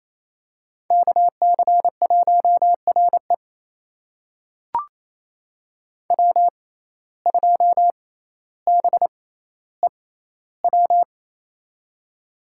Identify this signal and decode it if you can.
Morse code, heard as a keyed tone, about 28 wpm, spaced as for 14 wpm, 700 Hz. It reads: KC1RE W2BEW
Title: USA Call Signs - 5 Characters - 6x Character Spacing - 28wpm